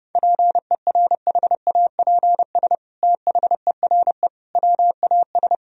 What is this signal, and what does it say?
Morse code, heard as a keyed tone, about 30 wpm, 700 Hz. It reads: PERHAPS THERE WAS
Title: Sentences from Top 1000 Words - Review Part 4 - 30wpm